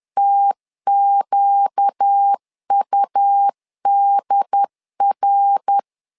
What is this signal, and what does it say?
Morse code, heard as a keyed tone, about 10 wpm, 800 Hz. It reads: TQUDR
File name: LEZIONI / 4 Q1S3 10wpm 2427